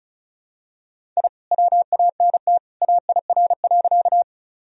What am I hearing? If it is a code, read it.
Morse code, heard as a keyed tone, about 36 wpm, 700 Hz. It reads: I WANT AIR.
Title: Sentences from Top 1000 Words - Review Part 1 - 35wpm